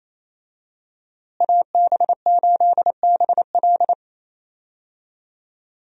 Morse code, heard as a keyed tone, about 28 wpm, 700 Hz. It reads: AB8BL
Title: USA Call Signs - 5 Characters - Repeated 3x - 28wpm